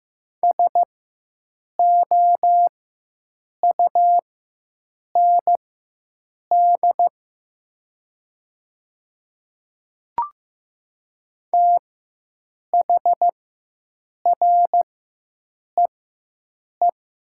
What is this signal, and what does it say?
Morse code, heard as a keyed tone, about 15 wpm, spaced as for 7 wpm, 700 Hz. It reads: SOUND THREE
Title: 4x Character Spacing - Top 200 Words - 15wpm